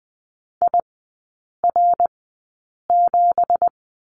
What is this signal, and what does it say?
Morse code, heard as a keyed tone, about 20 wpm, spaced as for 8 wpm, 690 Hz. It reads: IR7
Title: 166 Lesson 27e - Number 7 - Three Characters 20wpm